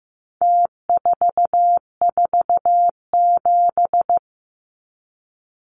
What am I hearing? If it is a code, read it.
Morse code, heard as a keyed tone, about 15 wpm, 700 Hz. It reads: T447
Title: Numbers - Contest Style - 4-Digit 15wpm